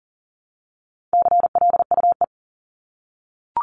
Morse code, heard as a keyed tone, about 40 wpm, 700 Hz. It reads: CLUE
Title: Top 1751-2000 Words 40wpm